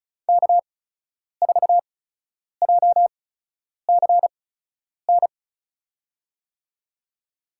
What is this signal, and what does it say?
Morse code, heard as a keyed tone, about 36 wpm, spaced as for 10 wpm, 700 Hz. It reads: K4JCN
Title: USA Call Signs - 5 Characters - 8x Character Spacing - 35wpm